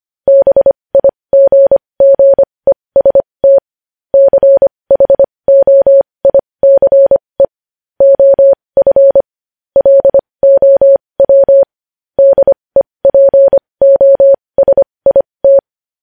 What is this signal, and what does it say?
Morse code, heard as a keyed tone, about 25 wpm, spaced as for 20 wpm, 550 Hz. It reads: BIGGEST CHOICE OF LOW DEPOSIT
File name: podcast / News headlines-1x-25x20